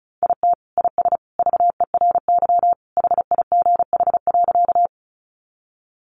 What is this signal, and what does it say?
Morse code, heard as a keyed tone, about 34 wpm, 700 Hz. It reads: IT IS VERY HIGH.
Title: Sentences from Top 200 Words 35wpm